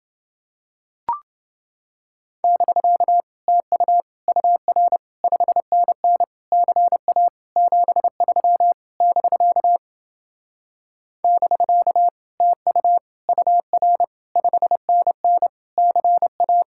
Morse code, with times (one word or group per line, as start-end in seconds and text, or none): 2.4-3.2 <BK>
3.5-4.0 TU
4.3-5.0 UR
5.2-6.2 5NN
6.5-7.3 CA
7.6-8.7 73
9.0-9.8 <BK>
11.2-12.1 <BK>
12.4-13.0 TU
13.3-14.0 UR
14.4-15.5 5NN
15.8-16.6 CA